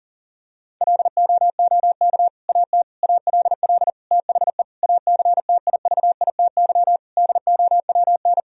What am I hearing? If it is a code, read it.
Morse code, heard as a keyed tone, about 40 wpm, 700 Hz. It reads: LOOK AT ALL THE ACTIVITY DOWN
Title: Sentences from Top 1000 Words - Review Part 4 - 40wpm